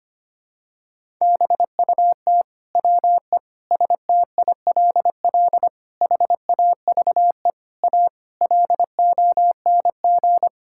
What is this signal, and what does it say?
Morse code, heard as a keyed tone, about 25 wpm, 700 Hz. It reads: BUT WE STILL HAVE A LONG
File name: Sets of 9 Words - 25wpm